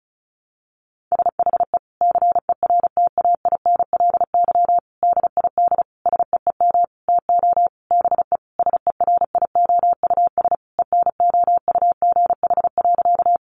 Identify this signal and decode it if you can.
Morse code, heard as a keyed tone, about 34 wpm, 700 Hz. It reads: SHE CERTAINLY DID SEEM TO BE SERIOUS ENOUGH.
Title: Sentences from Top 1100 Words 35wpm